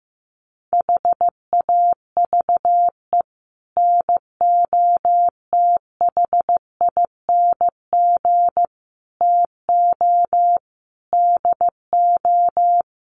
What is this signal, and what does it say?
Morse code, heard as a keyed tone, about 15 wpm, 700 Hz. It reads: HAVE NOTHING TO DO
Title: Sets of 5 Words 15wpm